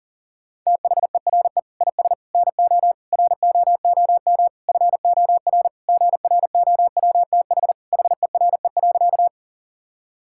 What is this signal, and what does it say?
Morse code, heard as a keyed tone, about 40 wpm, 700 Hz. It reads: THERE IS NO ROOM FOR GROWTH HERE.
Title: Sentences from Top 1000 Words - Review Part 5 - 40wpm